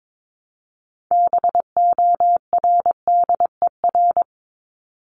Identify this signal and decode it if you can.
Morse code, heard as a keyed tone, about 22 wpm, 700 Hz. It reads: BORDER
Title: Top 1751-2000 Words 22wpm